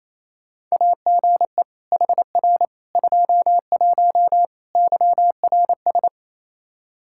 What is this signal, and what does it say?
Morse code, heard as a keyed tone, about 28 wpm, 700 Hz. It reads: AGE HR 21 YRS
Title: K1USN SST QSO Practice 28wpm